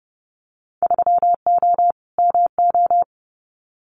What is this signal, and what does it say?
Morse code, heard as a keyed tone, about 30 wpm, 700 Hz. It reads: 3O MO
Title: Winter Field Day Exchanges Encore 30wpm